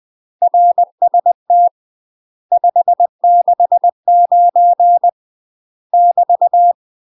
Morse code, heard as a keyed tone, about 20 wpm, 700 Hz. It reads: RST 569 <BT>
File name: UK MASTER QSOs TRAINING / 20WPM / UK MASTER QSOs 0047